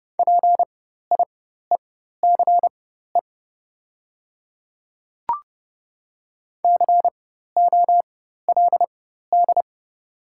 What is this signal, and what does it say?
Morse code, heard as a keyed tone, about 29 wpm, spaced as for 14 wpm, 700 Hz. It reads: PIECE COLD
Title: 4x Character Spacing - Top 500 Words - 30wpm